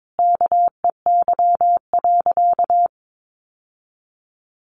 Morse code, heard as a keyed tone, about 22 wpm, 700 Hz. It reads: KEY.
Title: Sentences from Top 600 Words 22wpm